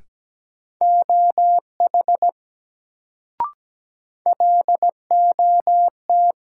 Morse code, heard as a keyed tone, about 17 wpm, 700 Hz. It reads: OH LOT
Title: Sentences from Top 500 Words - New Word Drill - 17wpm